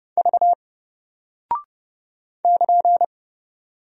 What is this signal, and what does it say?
Morse code, heard as a keyed tone, about 30 wpm, 700 Hz. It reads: V <KN>
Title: Single Character (Letter:Number:Punctuation:Prosign) Alt 30wpm